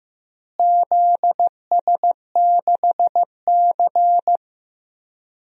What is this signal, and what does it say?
Morse code, heard as a keyed tone, about 15 wpm, 700 Hz. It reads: ZS6C
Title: DXCC-100 Call Signs - 4 Characters - Repeated 3x - 15wpm